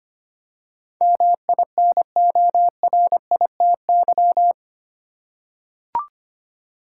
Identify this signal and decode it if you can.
Morse code, heard as a keyed tone, about 25 wpm, 700 Hz. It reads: MINORITY
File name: 8 Letter Words - First 3 Letters Given - 25wpm